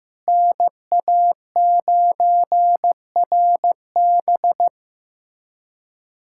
Morse code, heard as a keyed tone, about 15 wpm, 700 Hz. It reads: NA9RB
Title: USA Callsign DE Callsign - 5 Characters - 15wpm